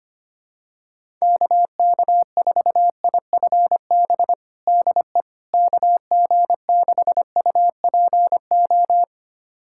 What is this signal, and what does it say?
Morse code, heard as a keyed tone, about 25 wpm, 700 Hz. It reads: KK4IFB DE KG6UPO
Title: USA Callsign DE Callsign - 6 Characters - 25wpm